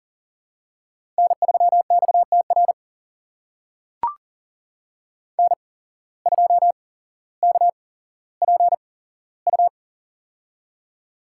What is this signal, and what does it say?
Morse code, heard as a keyed tone, about 40 wpm, spaced as for 17 wpm, 700 Hz. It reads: N3XTR N2KPU
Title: USA Call Signs - 5 Characters - 8x Character Spacing - 40wpm